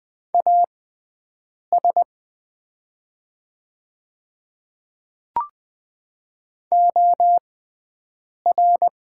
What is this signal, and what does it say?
Morse code, heard as a keyed tone, about 20 wpm, spaced as for 7 wpm, 700 Hz. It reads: AS OR
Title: 6x Character Spacing - Top 100 Words - 20wpm